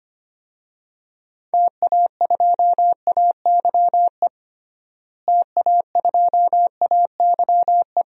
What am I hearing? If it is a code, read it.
Morse code, heard as a keyed tone, about 25 wpm, 700 Hz. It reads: TA2AYE TA2AYE
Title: DXCC-100 Call Signs - 6 Characters - Repeated 2x - 25wpm